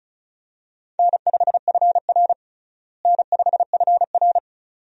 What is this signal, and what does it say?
Morse code, heard as a keyed tone, about 34 wpm, 700 Hz. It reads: N5FR N5FR
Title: USA Call Signs - 4 Characters - Repeated 2x - 35wpm